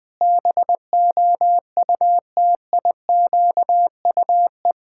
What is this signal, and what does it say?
Morse code, heard as a keyed tone, about 20 wpm, 700 Hz. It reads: BOUTIQUE
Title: Pangram Challenge v2 - 20wpm